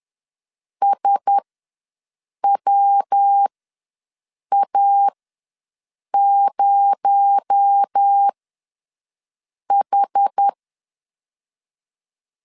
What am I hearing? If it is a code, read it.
Morse code, heard as a keyed tone, about 10 wpm, spaced as for 6 wpm, 800 Hz. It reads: SWA0H
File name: LEZIONI / 9 286 ws 10wpm 4032